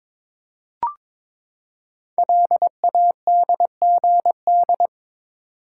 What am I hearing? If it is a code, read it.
Morse code, heard as a keyed tone, about 22 wpm, 700 Hz. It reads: LADGD
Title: Code Groups - 5 Letters - 22wpm